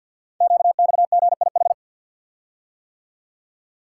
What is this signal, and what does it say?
Morse code, heard as a keyed tone, about 50 wpm, 700 Hz. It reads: QXGIH
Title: Code Groups - 5 Letters - 50wpm